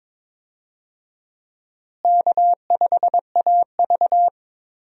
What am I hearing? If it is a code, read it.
Morse code, heard as a keyed tone, about 22 wpm, 700 Hz. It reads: K5AV
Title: USA Call Signs - 4 Characters - Repeated 3x - 22wpm